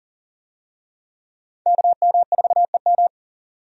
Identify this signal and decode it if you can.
Morse code, heard as a keyed tone, about 40 wpm, 700 Hz. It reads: KM4EM
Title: USA Call Signs - 5 Characters - 40wpm